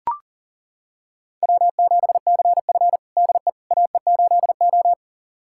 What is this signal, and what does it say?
Morse code, heard as a keyed tone, about 40 wpm, 700 Hz. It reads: W7CF DE AE8O
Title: USA Callsign DE Callsign - 4 Characters - 40wpm